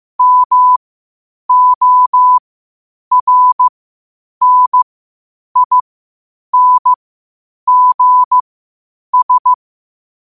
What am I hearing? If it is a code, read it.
Morse code, heard as a keyed tone, about 15 wpm, spaced as for 8 wpm, 1000 Hz. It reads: MORNINGS
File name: qotdfiles / 8wpm